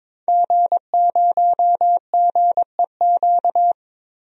Morse code, heard as a keyed tone, about 22 wpm, 700 Hz. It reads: G0GEQ
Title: DXCC-100 Call Signs - 5 Characters - 22wpm